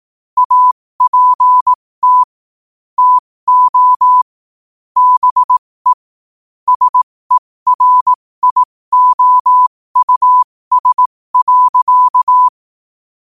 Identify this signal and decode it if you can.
Morse code, heard as a keyed tone, about 18 wpm, spaced as for 15 wpm, 1000 Hz. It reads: APT TO BE SERIOUS.
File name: qotdfiles / 15wpm